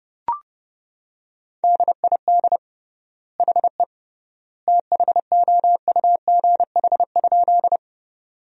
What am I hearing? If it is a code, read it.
Morse code, heard as a keyed tone, about 30 wpm, 700 Hz. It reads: DID HE THOUGH?
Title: 3x Word-Spacing - Sentences from Top 400 Words - 30wpm